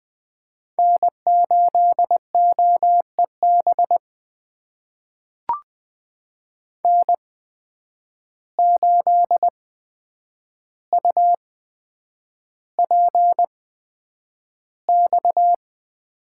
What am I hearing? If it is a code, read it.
Morse code, heard as a keyed tone, about 20 wpm, spaced as for 9 wpm, 700 Hz. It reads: N8OEB N8UPX
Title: USA Call Signs - 5 Characters - 8x Character Spacing - 20wpm